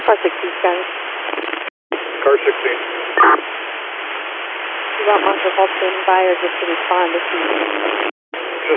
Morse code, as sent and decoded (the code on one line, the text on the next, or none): none